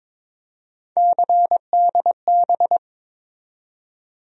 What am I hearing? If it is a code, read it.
Morse code, heard as a keyed tone, about 22 wpm, 700 Hz. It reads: CDB
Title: USA Call Sign Postfixes - 3 Letters 22wpm